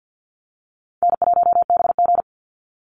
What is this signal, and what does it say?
Morse code, heard as a keyed tone, about 50 wpm, 700 Hz. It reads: N1BG
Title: Easy Call Signs - 50wpm